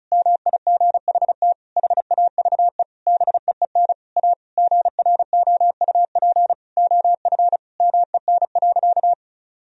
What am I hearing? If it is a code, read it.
Morse code, heard as a keyed tone, about 36 wpm, 700 Hz. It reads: MIGHT HAVE BEEN A GROUP OF MEN.
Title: Sentences from Top 1000 Words - Review Part 1 - 35wpm